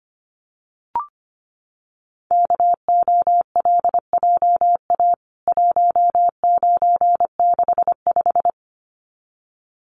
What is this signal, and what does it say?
Morse code, heard as a keyed tone, about 25 wpm, 700 Hz. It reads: KOLJA 1965
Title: CWT Exchange 25wpm